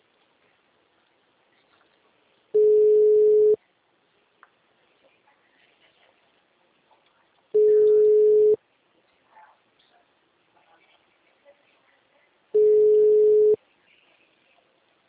Ringback tone (Europe, 425 Hz).